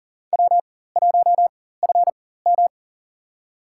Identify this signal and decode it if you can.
Morse code, heard as a keyed tone, about 40 wpm, spaced as for 19 wpm, 700 Hz. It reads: W1FM
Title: USA Call Signs - 4 Characters - 4x Character Spacing - 40wpm